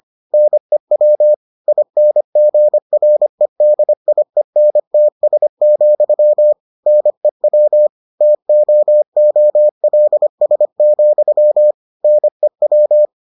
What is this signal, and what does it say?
Morse code, heard as a keyed tone, about 25 wpm, 600 Hz. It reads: NEW INGREDIENTS, NEW TOOLS, NEW